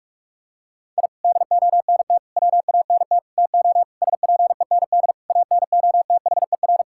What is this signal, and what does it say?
Morse code, heard as a keyed tone, about 41 wpm, 700 Hz. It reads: I DONT WANT TO SPEND ANOTHER
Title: Sentences from Top 1000 Words - Review Part 5 - 45wpm